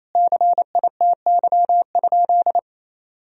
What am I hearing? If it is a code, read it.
Morse code, heard as a keyed tone, about 28 wpm, 700 Hz. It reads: CITY?
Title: Sentences from Top 1000 Words - Review Part 3 - 28wpm